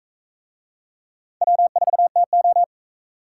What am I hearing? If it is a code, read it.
Morse code, heard as a keyed tone, about 36 wpm, 700 Hz. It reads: W4TO